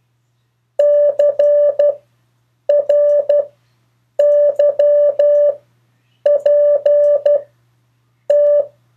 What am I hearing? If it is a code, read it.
Morse code, heard as a keyed tone, about 12 wpm, spaced as for 7 wpm, 580 Hz. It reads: CRYPT
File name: 2of4-words-1